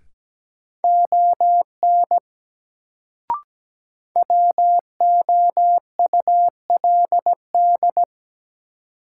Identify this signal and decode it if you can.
Morse code, heard as a keyed tone, about 17 wpm, 700 Hz. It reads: ON WOULD
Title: Sets of 2 Words - New Word Drill - 17wpm